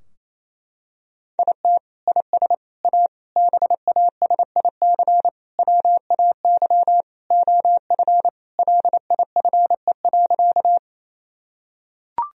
Morse code, heard as a keyed tone, about 28 wpm, 700 Hz. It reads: IT IS A BASIC WAY OF LIFE.
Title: Sentences from Top 1000 Words - Review Part 3 - 28wpm